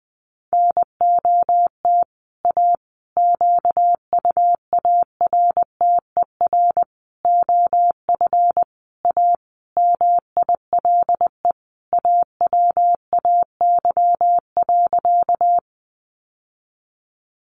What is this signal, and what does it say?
Morse code, heard as a keyed tone, about 20 wpm, 700 Hz. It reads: NOT A QUARTER OF A MILE AWAY.
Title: Sentences from Top 1300 Words 20wpm